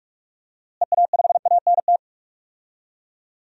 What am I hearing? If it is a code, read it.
Morse code, heard as a keyed tone, about 46 wpm, 700 Hz. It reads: EA5ANT
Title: DXCC-100 Call Signs - 6 Characters - Repeated 2x - 45wpm